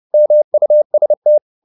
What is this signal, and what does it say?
Morse code, heard as a keyed tone, about 30 wpm, 600 Hz. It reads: MUST
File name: the old man and the sea / the old man and the sea 1 30wpm